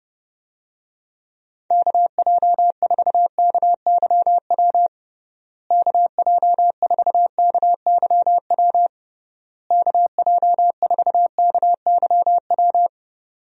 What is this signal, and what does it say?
Morse code, heard as a keyed tone, about 32 wpm, 700 Hz. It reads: KJ4KYW KJ4KYW KJ4KYW
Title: USA Call Signs - 6 Characters - Repeated 3x - 30wpm